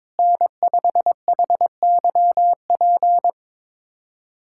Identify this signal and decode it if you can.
Morse code, heard as a keyed tone, about 22 wpm, 700 Hz. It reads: N5HYP
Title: USA Callsign DE Callsign - 5 Characters - 22wpm